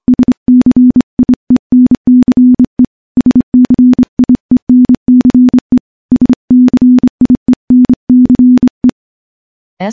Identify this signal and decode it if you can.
Morse code, heard as a keyed tone, about 25 wpm, 250 Hz. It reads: SCIENCE SCIENCE SCIENCE
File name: morse-training-25WPM